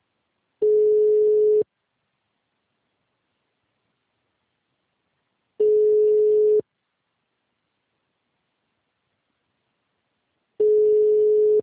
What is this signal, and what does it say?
Ringback tone (Europe, 425 Hz).
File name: out- -2001- - - .85